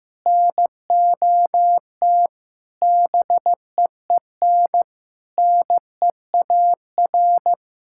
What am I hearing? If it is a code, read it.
Morse code, heard as a keyed tone, about 15 wpm, 700 Hz. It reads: NOT BEEN NEAR
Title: Sentences from Top 1000 Words - Review Part 6 - 15wpm